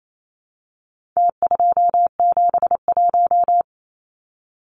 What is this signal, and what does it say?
Morse code, heard as a keyed tone, about 28 wpm, 700 Hz. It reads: T271
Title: Numbers - Contest Style - 4-Digit 28wpm